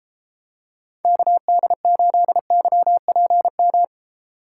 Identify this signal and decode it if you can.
Morse code, heard as a keyed tone, about 32 wpm, 700 Hz. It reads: KD8YPM